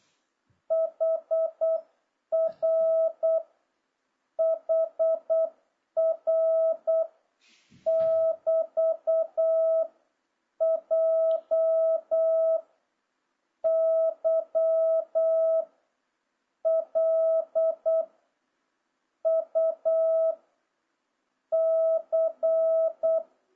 Morse code, heard as a keyed tone, about 8 wpm, 650 Hz. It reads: HR HR <BT> J Y L U C
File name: jh8cbh / 2026cw